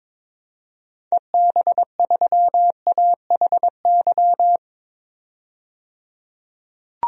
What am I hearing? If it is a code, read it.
Morse code, heard as a keyed tone, about 22 wpm, 700 Hz. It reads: EB3AHY
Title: DXCC-100 Call Signs - 6 Characters - 22wpm